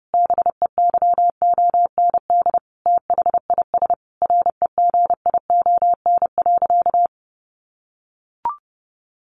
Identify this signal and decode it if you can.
Morse code, heard as a keyed tone, about 30 wpm, 700 Hz. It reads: BEYOND THIS REGION.
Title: Sentences from Top 1500 Words 30wpm